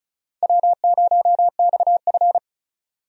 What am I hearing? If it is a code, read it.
Morse code, heard as a keyed tone, about 36 wpm, 700 Hz. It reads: W0XF
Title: USA Call Signs - 4 Characters - Repeated 3x - 35wpm